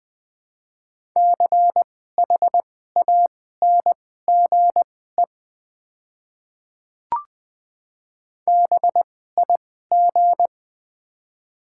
Morse code, heard as a keyed tone, about 20 wpm, spaced as for 15 wpm, 700 Hz. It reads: CHANGE BIG
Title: 2x Character Spacing - Top 500 Words - 20wpm